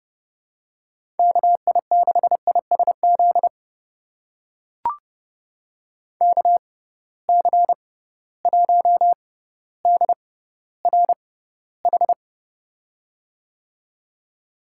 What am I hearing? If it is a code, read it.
Morse code, heard as a keyed tone, about 30 wpm, spaced as for 15 wpm, 700 Hz. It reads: KI6ISZ KC1DRH
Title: USA Call Signs - 6 Characters - 6x Character Spacing - 30wpm